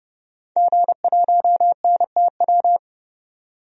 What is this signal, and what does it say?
Morse code, heard as a keyed tone, about 30 wpm, 690 Hz. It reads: G1NTW
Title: DXCC-100 Call Signs - 5 Characters - 30wpm